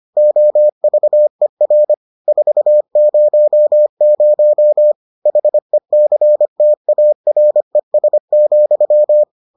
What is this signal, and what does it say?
Morse code, heard as a keyed tone, about 25 wpm, 600 Hz. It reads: OVER 400 HECTARES,